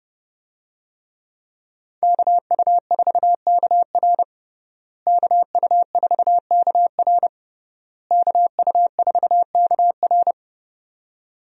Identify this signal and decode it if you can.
Morse code, heard as a keyed tone, about 30 wpm, 700 Hz. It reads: KU4KR KU4KR KU4KR
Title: USA Call Signs - 5 Characters - Repeated 3x - 30wpm